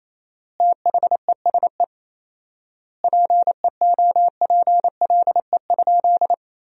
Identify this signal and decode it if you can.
Morse code, heard as a keyed tone, about 28 wpm, 700 Hz. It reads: THESE PEOPLE?
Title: 4x Word-Spacing - Sentences from Top 200 Words - 28wpm